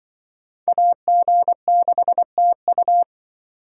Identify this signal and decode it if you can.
Morse code, heard as a keyed tone, about 24 wpm, 700 Hz. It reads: AG6TU
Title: Speed Racing - USA 2x2 Call Signs 20wpm